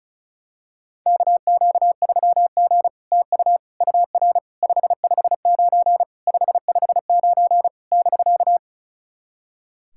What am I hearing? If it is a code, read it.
Morse code, heard as a keyed tone, about 36 wpm, 700 Hz. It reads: KQ3G TU UR 559 559 <BK>
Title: SOTA QSO Practice 35wpm